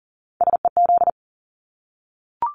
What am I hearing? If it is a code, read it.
Morse code, heard as a keyed tone, about 40 wpm, 700 Hz. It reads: SEZ
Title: Abbreviations Repeated 3x - 40wpm